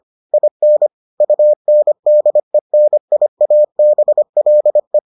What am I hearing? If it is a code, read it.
Morse code, heard as a keyed tone, about 25 wpm, 600 Hz. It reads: IN UNDENIABLE